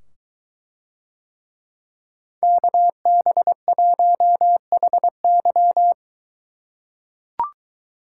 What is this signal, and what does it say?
Morse code, heard as a keyed tone, about 23 wpm, 700 Hz. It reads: KB1HY